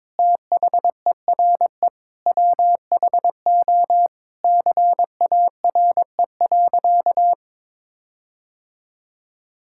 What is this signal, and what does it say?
Morse code, heard as a keyed tone, about 22 wpm, 700 Hz. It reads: THERE WHO CARE.
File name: Sentences with 9-Word Phrases - 22wpm